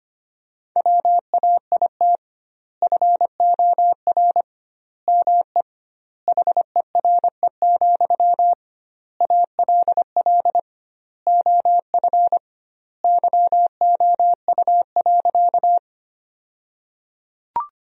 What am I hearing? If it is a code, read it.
Morse code, heard as a keyed tone, about 25 wpm, 700 Hz. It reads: WAIT FOR ME HERE, ALL OF YOU.
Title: 2x Word-Spacing - Sentences from Top 500 Words - 25wpm